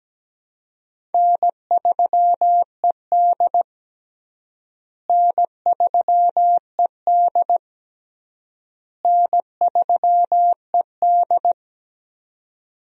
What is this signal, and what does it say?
Morse code, heard as a keyed tone, about 17 wpm, 700 Hz. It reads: N3ED N3ED N3ED
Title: USA Call Signs - 4 Characters - Repeated 3x - 17wpm